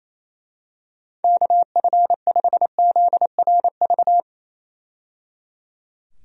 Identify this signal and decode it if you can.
Morse code, heard as a keyed tone, about 28 wpm, 700 Hz. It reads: KF5ZRV
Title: USA Call Signs - 6 Characters - 28wpm